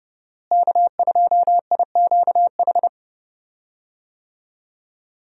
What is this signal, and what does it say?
Morse code, heard as a keyed tone, about 30 wpm, 700 Hz. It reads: K2IQH
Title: USA Call Signs - 5 Characters - Repeated 3x - 30wpm